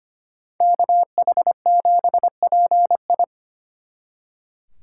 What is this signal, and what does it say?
Morse code, heard as a keyed tone, about 30 wpm, 700 Hz. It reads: KH7PI
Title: Speed Racing - USA 2x2 Call Signs 25wpm